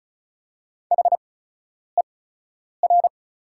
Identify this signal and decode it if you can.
Morse code, heard as a keyed tone, about 34 wpm, spaced as for 10 wpm, 700 Hz. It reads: HER
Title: 8x Character Spacing - Top 300 Words - 35wpm